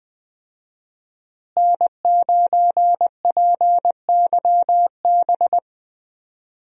Morse code, heard as a keyed tone, about 20 wpm, 700 Hz. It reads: N9PYB
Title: USA Call Signs - 5 Characters - Repeated 3x - 20wpm